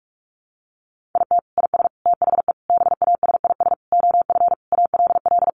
Morse code, heard as a keyed tone, about 44 wpm, 700 Hz. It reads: IT IS THE BASIS OF ALL
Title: Sentences from Top 1200 Words 45wpm